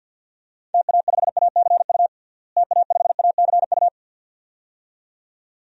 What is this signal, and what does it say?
Morse code, heard as a keyed tone, about 46 wpm, 700 Hz. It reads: TA5ACU TA5ACU
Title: DXCC-100 Call Signs - 6 Characters - Repeated 2x - 50wpm